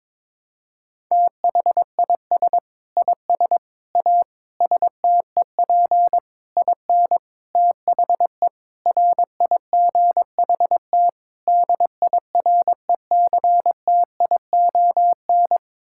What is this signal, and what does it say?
Morse code, heard as a keyed tone, about 22 wpm, 700 Hz. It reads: THIS IS A STEP IN THE RIGHT DIRECTION
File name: practice / 22wpm / Sets of 8 Words - 22wpm